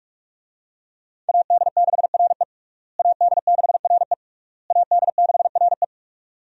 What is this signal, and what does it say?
Morse code, heard as a keyed tone, about 41 wpm, 700 Hz. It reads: AD6RE AD6RE AD6RE
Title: USA Call Signs - 5 Characters - Repeated 3x - 45wpm